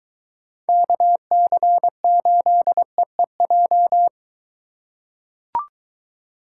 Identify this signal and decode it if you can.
Morse code, heard as a keyed tone, about 23 wpm, 700 Hz. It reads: KC8EEJ